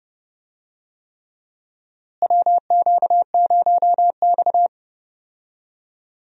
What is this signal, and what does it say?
Morse code, heard as a keyed tone, about 32 wpm, 700 Hz. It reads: WQ0X
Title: USA Call Signs - 4 Characters - Repeated 3x - 30wpm